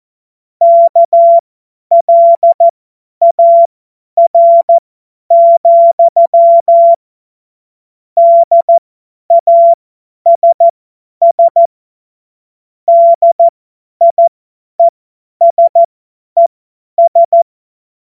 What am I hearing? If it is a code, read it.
Morse code, heard as a keyed tone, about 14 wpm, spaced as for 10 wpm, 680 Hz. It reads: KLAR, DASS DIESES
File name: Witz-05-August-2025-50-F70